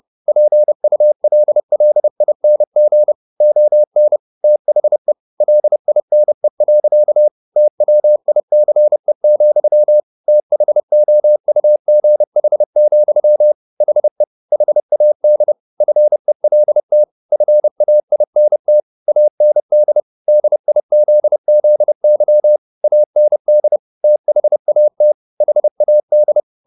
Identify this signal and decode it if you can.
Morse code, heard as a keyed tone, about 30 wpm, 600 Hz. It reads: PULLING ON THE LINE. TWICE, THOUGH, HE HAD FELT FAINT AND DIZZY AND THAT HAD